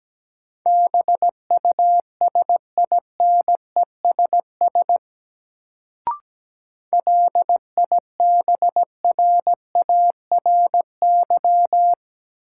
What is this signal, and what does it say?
Morse code, heard as a keyed tone, about 17 wpm, 700 Hz. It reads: BUSINESS LIBRARY
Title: Sentences from Top 600 Words - New Word Drill - 17wpm